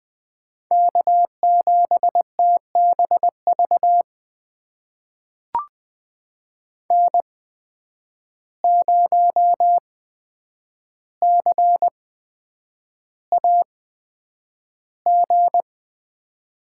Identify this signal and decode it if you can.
Morse code, heard as a keyed tone, about 20 wpm, spaced as for 9 wpm, 700 Hz. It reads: K7TBV N0CAG
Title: USA Call Signs - 5 Characters - 8x Character Spacing - 20wpm